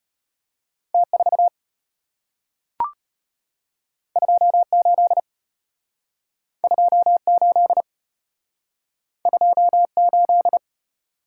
Morse code, heard as a keyed tone, about 34 wpm, 700 Hz. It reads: T4 28 28 28